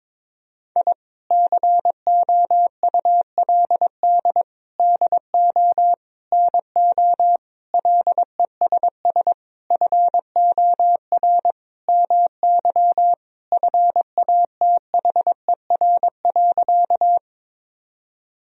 Morse code, heard as a keyed tone, about 22 wpm, 700 Hz. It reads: I COULD DO NO LESS FOR MY FATHER.
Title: Sentences from Top 1000 Words - Review Part 2 - 22wpm